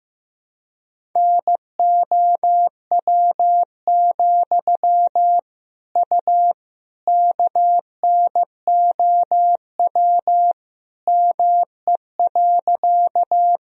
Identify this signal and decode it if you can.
Morse code, heard as a keyed tone, about 15 wpm, 700 Hz. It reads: NOW, U KNOW ME.
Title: Sentences from Top 200 Words and Abbreviations 15wpm